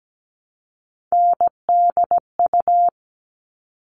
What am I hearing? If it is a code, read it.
Morse code, heard as a keyed tone, about 17 wpm, 700 Hz. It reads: NDU
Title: All Cut Numbers - 3-Digit 17wpm